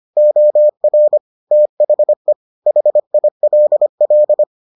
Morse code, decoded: OR THE HILL

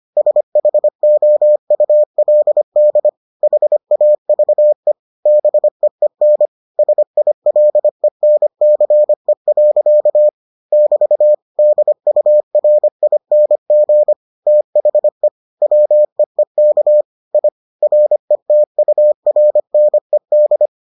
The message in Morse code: SHOULD HAVE BEEN SILENCE. <BT> DURING THE WEEK I RETURNED

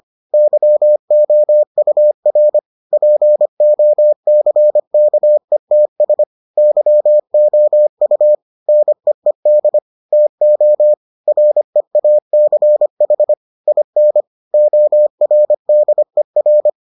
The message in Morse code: YOUR POCKETS YOU NEED TO REACH IN ORDER

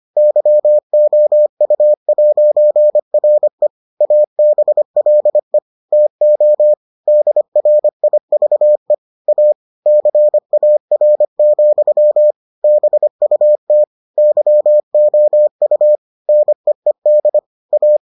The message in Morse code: YOU'RE ABLE TO DRIVE A CAR, BUT YOU NEED A

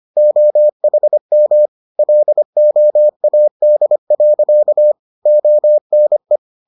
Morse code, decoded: OHM LOAD. ONE

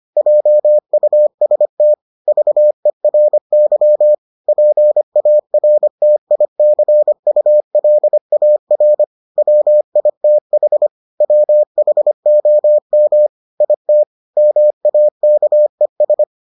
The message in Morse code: JUST VERY PARTICULAR WITH WHOM IT MAKES

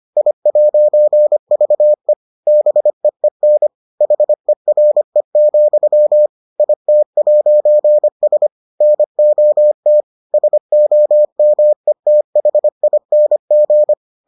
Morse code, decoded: I'VE BEEN HERE, IT'S NOT SOMETHING

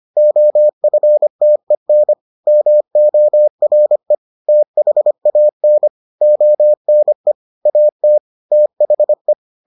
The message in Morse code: OFTEN MORE THAN ONE AT THE

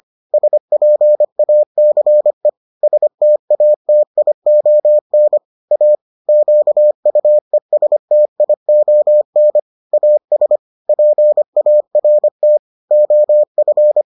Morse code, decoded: SPACE STATION A QUESTION AS PART OF

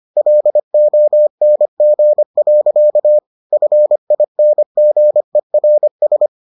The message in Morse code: LONG. FINGERS

